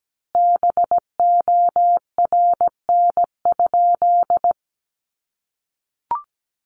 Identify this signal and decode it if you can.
Morse code, heard as a keyed tone, about 17 wpm, 700 Hz. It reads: BORN?
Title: Sentences from Top 1000 Words 17wpm